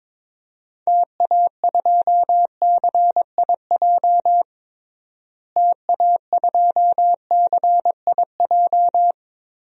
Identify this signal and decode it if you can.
Morse code, heard as a keyed tone, about 21 wpm, 700 Hz. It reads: TA2CIJ TA2CIJ
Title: DXCC-100 Call Signs - 6 Characters - Repeated 3x - 22wpm